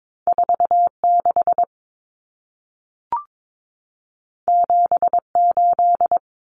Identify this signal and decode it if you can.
Morse code, heard as a keyed tone, about 22 wpm, 700 Hz. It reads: 46 78
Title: Numbers - 2-Digit 22wpm